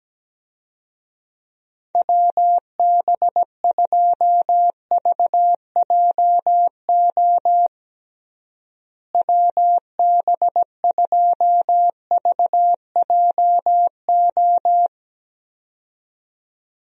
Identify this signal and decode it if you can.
Morse code, heard as a keyed tone, about 17 wpm, 700 Hz. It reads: WB2VJO WB2VJO
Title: USA Call Signs - 6 Characters - Repeated 2x - 17wpm